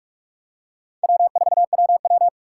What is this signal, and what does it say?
Morse code, heard as a keyed tone, about 47 wpm, 700 Hz. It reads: W4WW
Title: USA Call Signs - 4 Characters - 3x Character Spacing - 45wpm